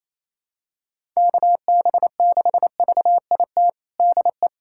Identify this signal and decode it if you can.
Morse code, heard as a keyed tone, about 28 wpm, 700 Hz. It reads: KB6VIT DE
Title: USA Callsign DE Callsign - 6 Characters - 28wpm